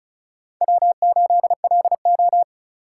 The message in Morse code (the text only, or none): W8LO